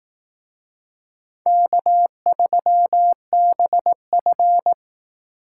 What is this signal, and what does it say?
Morse code, heard as a keyed tone, about 18 wpm, 700 Hz. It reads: K3BF
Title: Speed Racing - USA 1x2 Call Signs 15wpm